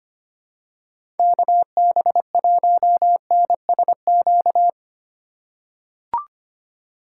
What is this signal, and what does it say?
Morse code, heard as a keyed tone, about 25 wpm, 700 Hz. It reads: KB1NSQ
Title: USA Call Signs - 4x Character Spacing - 25wpm